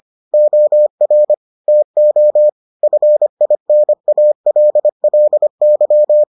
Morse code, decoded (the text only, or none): OR TO FINALLY